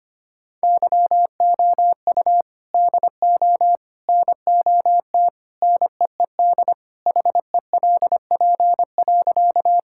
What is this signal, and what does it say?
Morse code, heard as a keyed tone, about 25 wpm, 700 Hz. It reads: YOU DO NOT NEED HELP.
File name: practice / 25wpm / Sentences with 3-Word Phrases - 25wpm